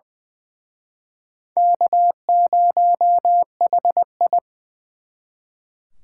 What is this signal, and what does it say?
Morse code, heard as a keyed tone, about 24 wpm, 700 Hz. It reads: K0HI
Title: Speed Racing - USA 1x2 Call Signs 20wpm